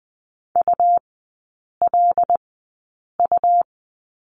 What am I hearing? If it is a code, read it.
Morse code, heard as a keyed tone, about 20 wpm, spaced as for 8 wpm, 700 Hz. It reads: ULU
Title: 107 Lesson 18e - Letter m - Three Characters 20wpm